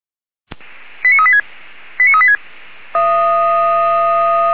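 Five-tone selective call (ZVEI), heard as quick runs of five tones, tones of about 70 ms. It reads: 89386 89386
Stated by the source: ZVEI five-tone selective call 89227 received on SDR by feuerwehr-orlamuende (fire brigade Orlamünde).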